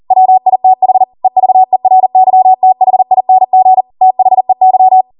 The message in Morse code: WITH EVERYTHING THEY